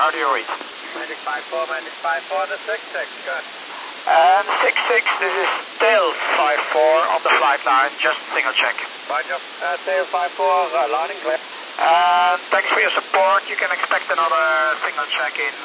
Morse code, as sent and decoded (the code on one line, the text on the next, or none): none